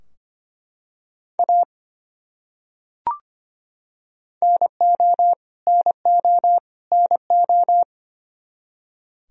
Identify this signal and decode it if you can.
Morse code, heard as a keyed tone, about 25 wpm, 700 Hz. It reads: A NO NO NO
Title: Top 100 Words Repeated 25wpm